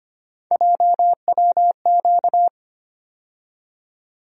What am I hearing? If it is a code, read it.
Morse code, heard as a keyed tone, about 25 wpm, 700 Hz. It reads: JWQ
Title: USA Call Sign Postfixes - 3 Letters 25wpm